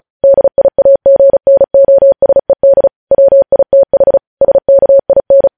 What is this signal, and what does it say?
Morse code, heard as a keyed tone, about 36 wpm, 550 Hz. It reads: DIAGNOSED WITH SKIN
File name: podcast / News headlines-1x-35x35